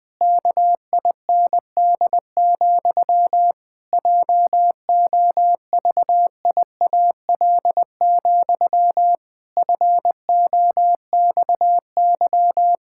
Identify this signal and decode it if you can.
Morse code, heard as a keyed tone, about 20 wpm, 700 Hz. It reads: KIND, JOVIAL, FOXY
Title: Pangram Challenge v2 - 20wpm